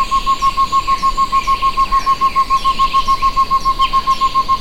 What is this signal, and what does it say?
There is no code.